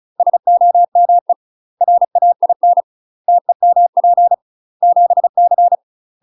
Morse code, heard as a keyed tone, about 36 wpm, 700 Hz. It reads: SOME RAIN TEMP 7C
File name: UK MASTER QSOs TRAINING / 35WPM / UK MASTER QSOs 0012